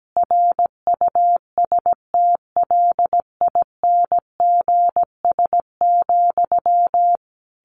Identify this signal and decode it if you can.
Morse code, heard as a keyed tone, about 17 wpm, 700 Hz. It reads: RUSTLINGS,
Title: A Sound Of Thunder 17wpm